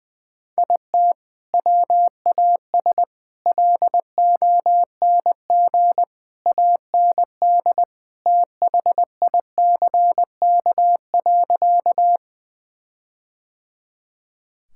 Morse code, heard as a keyed tone, about 20 wpm, 700 Hz. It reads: IT WAS LONG AND THICK.
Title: Sentences from Top 1600 Words 20wpm